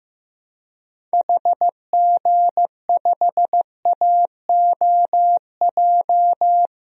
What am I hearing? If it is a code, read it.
Morse code, heard as a keyed tone, about 15 wpm, 700 Hz. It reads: HG5AOJ
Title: DXCC-100 Call Signs - 6 Characters - 15wpm